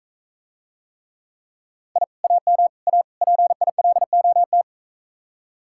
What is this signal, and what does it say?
Morse code, heard as a keyed tone, about 44 wpm, 700 Hz. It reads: I AM A PILOT